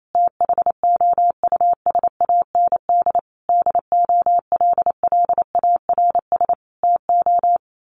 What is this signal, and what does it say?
Morse code, heard as a keyed tone, about 28 wpm, 700 Hz. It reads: THOUSAND DOLLARS TO
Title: A Sound Of Thunder 28wpm